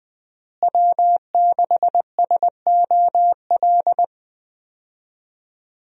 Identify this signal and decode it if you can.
Morse code, heard as a keyed tone, about 20 wpm, 700 Hz. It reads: W6SOL
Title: USA Call Signs - 5 Characters - Repeated 3x - 20wpm